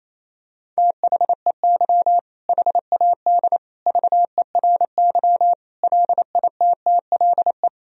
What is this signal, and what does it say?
Morse code, heard as a keyed tone, about 28 wpm, 700 Hz. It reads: THEY HAD VERY LITTLE
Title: Sentences from Top 1000 Words - Review Part 2 - 28wpm